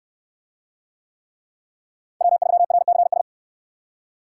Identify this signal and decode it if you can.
Morse code, heard as a keyed tone, about 65 wpm, 690 Hz. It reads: W3RCS